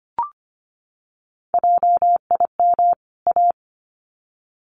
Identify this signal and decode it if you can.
Morse code, heard as a keyed tone, about 25 wpm, 700 Hz. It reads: JIM A
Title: CWT Exchange 25wpm